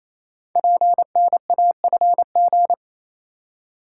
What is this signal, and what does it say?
Morse code, heard as a keyed tone, about 28 wpm, 700 Hz. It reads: PNAFG
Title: Code Groups - 5 Letters - 28wpm